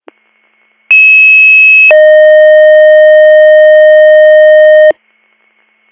Two-tone page, heard as a one-second tone, then a three-second tone, about 2600 then 620 Hz. It is heard from 0.9 to 4.9 s.